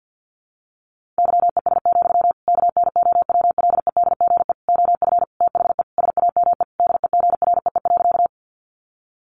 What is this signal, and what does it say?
Morse code, heard as a keyed tone, about 48 wpm, 700 Hz. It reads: YES, KNOWLEDGE OF THE SAME DEGREE.
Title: Sentences from Top 1500 Words 50wpm